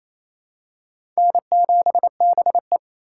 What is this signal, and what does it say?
Morse code, heard as a keyed tone, about 28 wpm, 700 Hz. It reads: N7BE
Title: USA Call Signs - 4 Characters - 4x Character Spacing - 28wpm